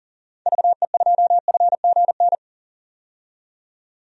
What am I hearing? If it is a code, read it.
Morse code, heard as a keyed tone, about 40 wpm, 700 Hz. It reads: VE2FGN
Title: DXCC-100 Call Signs - 6 Characters - 40wpm